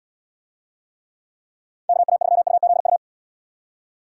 Morse code, heard as a keyed tone, about 78 wpm, 700 Hz. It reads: KA3UZU